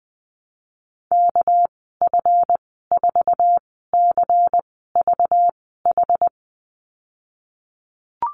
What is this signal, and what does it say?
Morse code, heard as a keyed tone, about 20 wpm, spaced as for 14 wpm, 700 Hz. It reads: KF4CVH
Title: 245 Lesson 39g - Letter z - Call Signs 20wpm